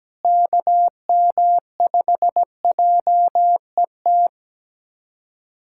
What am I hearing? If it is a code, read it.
Morse code, heard as a keyed tone, about 17 wpm, 700 Hz. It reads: KM5JET
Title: USA Callsign DE Callsign - 6 Characters - 17wpm